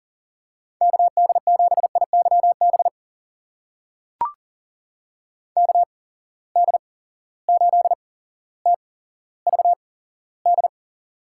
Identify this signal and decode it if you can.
Morse code, heard as a keyed tone, about 38 wpm, spaced as for 17 wpm, 700 Hz. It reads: KD7IYB KD8TVD
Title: USA Call Signs - 6 Characters - 8x Character Spacing - 40wpm